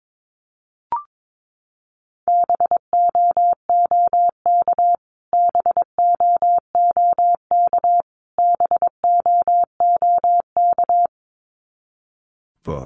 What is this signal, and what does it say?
Morse code, heard as a keyed tone, about 22 wpm, 700 Hz. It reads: BOOK BOOK BOOK
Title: Top 300 Words Repeated 22wpm